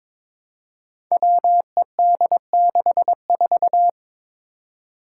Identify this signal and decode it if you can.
Morse code, heard as a keyed tone, about 22 wpm, 700 Hz. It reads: WED64
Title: Five Challenging Letter-Number Combinations 22wpm